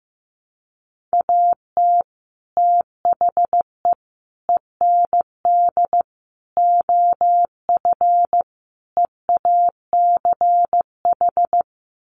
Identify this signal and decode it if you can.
Morse code, heard as a keyed tone, about 15 wpm, 700 Hz. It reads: AT THE END OF EACH
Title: Sets of 5 Words Encore 15wpm